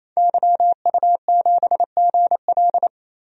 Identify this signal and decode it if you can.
Morse code, heard as a keyed tone, about 28 wpm, 700 Hz. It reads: YU7GL
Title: DXCC-100 Call Signs - 5 Characters - 28wpm